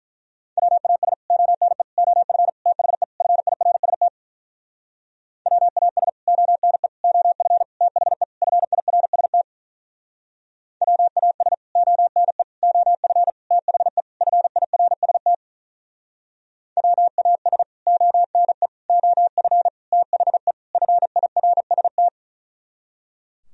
Morse code, heard as a keyed tone, about 46 wpm, 700 Hz. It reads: WAS ONE OF THE FIRST WAS ONE OF THE FIRST WAS ONE OF THE FIRST WAS ONE OF THE FIRST